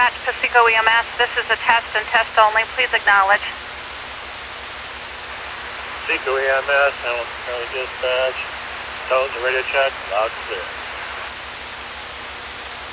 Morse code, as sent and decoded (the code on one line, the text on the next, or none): none